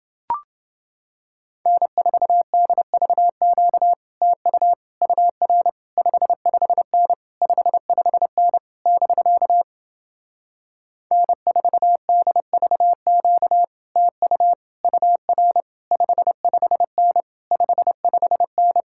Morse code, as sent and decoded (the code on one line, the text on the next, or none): -. ....- -.. ...- --.- / - ..- / ..- .-. / ..... ..... -. / ..... ..... -. / -...-.- / -. ....- -.. ...- --.- / - ..- / ..- .-. / ..... ..... -. / ..... ..... -.
N4DVQ TU UR 55N 55N <BK> N4DVQ TU UR 55N 55N